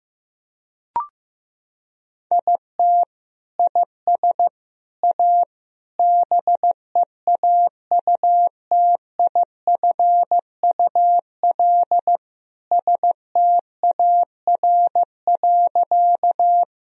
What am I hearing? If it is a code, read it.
Morse code, heard as a keyed tone, about 15 wpm, 700 Hz. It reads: IT IS A BEAUTIFUL STAR.
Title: Sentences from Top 1000 Words - Review Part 2 - 15wpm